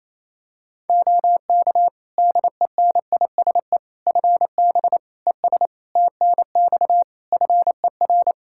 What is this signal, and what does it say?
Morse code, heard as a keyed tone, about 28 wpm, 700 Hz. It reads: OK DENISE FB ES TNX FER
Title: K1USN SST QSO Practice 28wpm